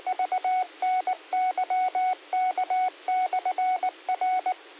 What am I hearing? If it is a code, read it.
Morse code, heard as a keyed tone, about 19 wpm, 1480 Hz, with radio noise. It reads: VNYK/R
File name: qsorec SimplexLogic